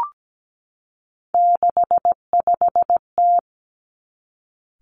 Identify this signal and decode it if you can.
Morse code, heard as a keyed tone, about 17 wpm, 700 Hz. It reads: 65T
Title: Three Challenging Letter-Number Combinations 17wpm